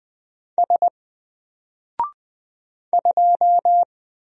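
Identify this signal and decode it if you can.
Morse code, heard as a keyed tone, about 20 wpm, 700 Hz. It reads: S 2
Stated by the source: Single letter and number Morse drill at 20 wpm, from N4LCD.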